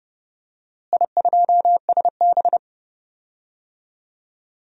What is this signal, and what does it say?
Morse code, heard as a keyed tone, about 30 wpm, 700 Hz. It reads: I2SB
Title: DXCC-100 Call Signs - 4 Characters - Repeated 3x - 30wpm